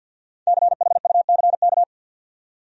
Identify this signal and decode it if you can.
Morse code, heard as a keyed tone, about 52 wpm, 700 Hz. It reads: CHUCK
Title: USA Common Names - 50wpm